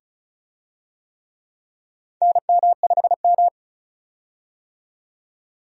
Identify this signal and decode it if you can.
Morse code, heard as a keyed tone, about 34 wpm, 700 Hz. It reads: NM5M
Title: USA Call Signs - 4 Characters - 35wpm